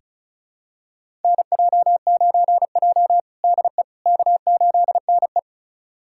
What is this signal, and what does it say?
Morse code, heard as a keyed tone, about 34 wpm, 700 Hz. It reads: NJ9J DE K8NE
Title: USA Callsign DE Callsign - 4 Characters - 35wpm